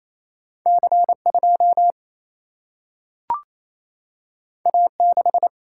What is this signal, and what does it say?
Morse code, heard as a keyed tone, about 28 wpm, 700 Hz. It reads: C 2 A 6
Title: Single-Letter Single-Number 28wpm